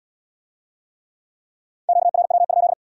Morse code, heard as a keyed tone, about 75 wpm, 700 Hz. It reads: QRW?